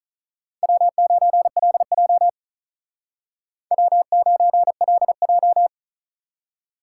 Morse code, heard as a keyed tone, about 46 wpm, 700 Hz. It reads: W9LJ W9LJ